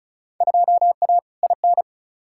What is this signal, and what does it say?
Morse code, heard as a keyed tone, about 34 wpm, 700 Hz. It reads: 2A IN
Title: Field Day Exchanges - 35wpm